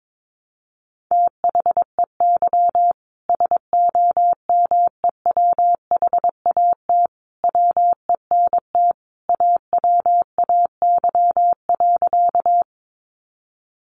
Morse code, heard as a keyed tone, about 22 wpm, 700 Hz. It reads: THEY SOMEWHAT WENT AWAY.
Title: Sentences from Top 1300 Words 22wpm